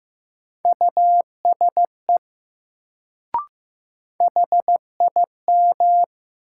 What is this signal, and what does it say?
Morse code, heard as a keyed tone, about 15 wpm, 700 Hz. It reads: USE HIM
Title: Sentences from Top 100 Words - New Word Drill - 15wpm